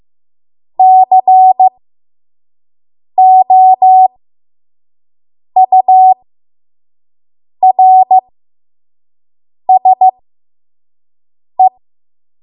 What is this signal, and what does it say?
Morse code, heard as a keyed tone, about 15 wpm, spaced as for 5 wpm, 750 Hz. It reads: COURSE
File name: Morse / 05WPM